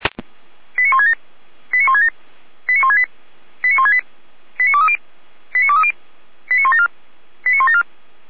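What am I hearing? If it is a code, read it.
Five-tone selective call (ZVEI), heard as quick runs of five tones, tones of about 70 ms. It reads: 89167 89167 89168 89168 89230 89230 89174 89174
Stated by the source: ZVEI five-tone call received by SDR.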